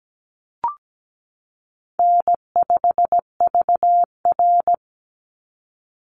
Morse code, heard as a keyed tone, about 17 wpm, 700 Hz. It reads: N5VR
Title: Easy Call Signs Repeated 3x - 17wpm